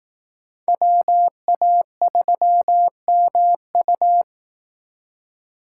Morse code, heard as a keyed tone, about 18 wpm, 700 Hz. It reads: WA3MU
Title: Speed Racing - USA 2x2 Call Signs 15wpm